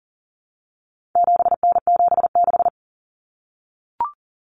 Morse code, heard as a keyed tone, about 40 wpm, 700 Hz. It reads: 7N76
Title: Common Cut Numbers - 4-Digit 40wpm